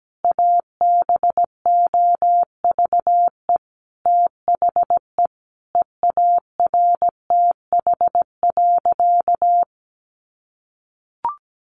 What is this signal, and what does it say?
Morse code, heard as a keyed tone, about 17 wpm, 700 Hz. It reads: ABOVE THE EARTH.
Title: A Sound Of Thunder 17wpm